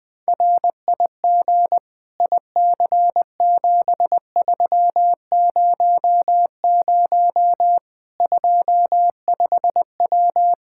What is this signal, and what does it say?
Morse code, heard as a keyed tone, about 21 wpm, 700 Hz. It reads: RIG IC7300 25W
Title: K1USN SST QSO Practice 20wpm